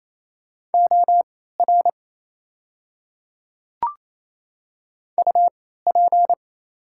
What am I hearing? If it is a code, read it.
Morse code, heard as a keyed tone, about 28 wpm, spaced as for 16 wpm, 690 Hz. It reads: OR UP
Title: 3x Character Spacing - Top 100 Words - 28wpm